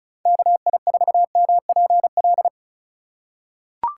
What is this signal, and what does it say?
Morse code, heard as a keyed tone, about 36 wpm, 700 Hz. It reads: KI4MPL
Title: USA Call Signs - 6 Characters - 4x Character Spacing - 35wpm